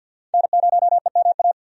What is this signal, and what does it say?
Morse code, heard as a keyed tone, about 48 wpm, 700 Hz. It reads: N0EMA
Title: USA Call Signs - 5 Characters - 50wpm